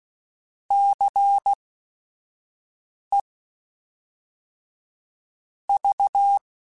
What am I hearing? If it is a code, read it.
Morse code, heard as a keyed tone, about 16 wpm, spaced as for 5 wpm, 800 Hz. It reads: CE V
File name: Technical Subjects / Technical Subjects-mp3s / XC @5